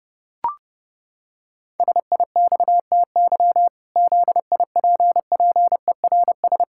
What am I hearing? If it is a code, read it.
Morse code, heard as a keyed tone, about 30 wpm, 700 Hz. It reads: SIXTY ZIPPERS
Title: Pangram Challenge v2 - 30wpm